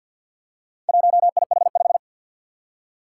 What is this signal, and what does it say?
Morse code, heard as a keyed tone, about 50 wpm, 700 Hz. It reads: 1IH5
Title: Four Challenging Letter-Number Combinations 50wpm